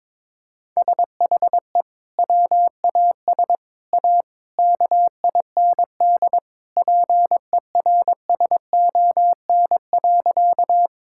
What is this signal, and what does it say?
Morse code, heard as a keyed tone, about 22 wpm, 700 Hz. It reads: SHE WAS A KIND PERSON.
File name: practice / 22wpm / Sentences with 3-Word Phrases - 22wpm